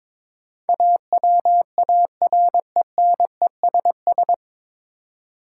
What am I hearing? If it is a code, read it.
Morse code, heard as a keyed tone, about 22 wpm, 700 Hz. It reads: AWARENESS
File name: 9 Letter Words - First 3 Letters Given - 22wpm